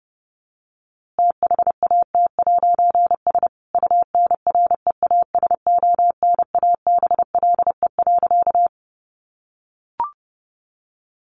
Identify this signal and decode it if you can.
Morse code, heard as a keyed tone, about 30 wpm, 700 Hz. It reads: THAT'S UNREASONABLE.
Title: A Sound Of Thunder 30wpm